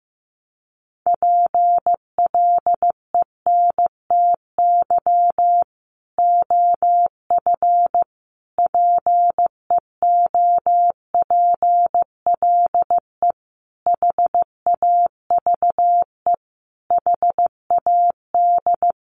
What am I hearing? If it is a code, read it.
Morse code, heard as a keyed tone, about 15 wpm, 700 Hz. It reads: PLENTY OF PEOPLE HAVE HAD
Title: Sentences from Top 1300 Words 15wpm